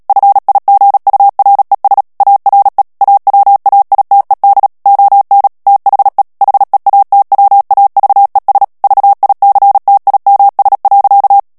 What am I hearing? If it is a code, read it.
Morse code, heard as a keyed tone, about 37 wpm, 800 Hz. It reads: FIGURES ARE AWAITED ON THE HEATWAVES VICTIMS.
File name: morse mp3 / ANGLAIS texte clair 2 vitesse 35 mots minute